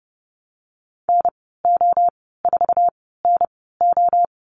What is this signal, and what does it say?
Morse code, heard as a keyed tone, about 30 wpm, spaced as for 17 wpm, 700 Hz. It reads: NO4NO
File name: Lesson 14g - Number 2 - Call Signs 30wpm